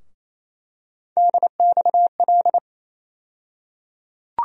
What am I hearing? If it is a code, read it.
Morse code, heard as a keyed tone, about 28 wpm, 700 Hz. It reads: DXL
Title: USA Call Sign Postfixes - 3 Letters 28wpm